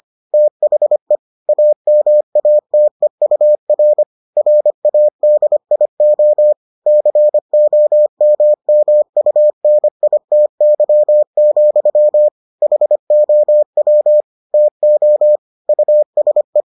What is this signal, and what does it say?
Morse code, heard as a keyed tone, about 25 wpm, 600 Hz. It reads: THE AMATEUR RADIO COMMUNITY, HOW TO USE